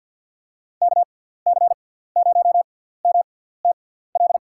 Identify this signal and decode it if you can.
Morse code, heard as a keyed tone, about 52 wpm, spaced as for 17 wpm, 700 Hz. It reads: KC0MTL
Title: USA Call Signs - 6 Characters - 6x Character Spacing - 50wpm